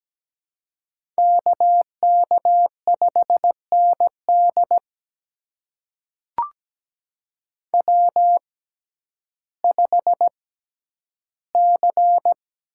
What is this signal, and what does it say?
Morse code, heard as a keyed tone, about 17 wpm, spaced as for 10 wpm, 700 Hz. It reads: KK5ND W5C
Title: USA Call Signs - 5 Characters - 6x Character Spacing - 17wpm